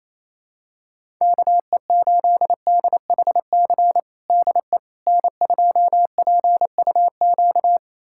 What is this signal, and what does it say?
Morse code, heard as a keyed tone, about 28 wpm, 700 Hz. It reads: KE8DHC DE N2PUQ
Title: USA Callsign DE Callsign 28wpm